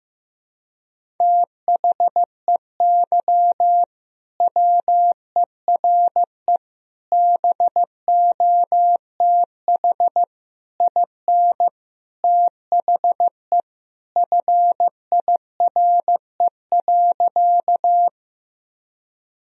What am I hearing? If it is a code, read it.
Morse code, heard as a keyed tone, about 16 wpm, 700 Hz. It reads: THEY WERE BOTH IN THE FIRE.
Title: Sentences from Top 1000 Words - Review Part 1 - 15wpm